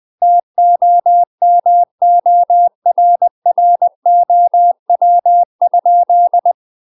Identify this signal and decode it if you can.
Morse code, heard as a keyed tone, about 20 wpm, 700 Hz. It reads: TOMORROW?
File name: UK MASTER QSOs TRAINING / 20WPM / UK MASTER QSOs 0022